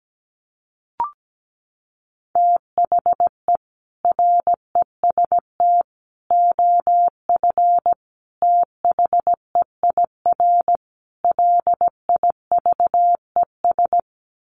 Morse code, heard as a keyed tone, about 17 wpm, 700 Hz. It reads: THE REST OF THEIR LIVES
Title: Sets of 5 Words Encore 17wpm